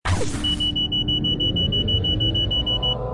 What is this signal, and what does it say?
There is no code.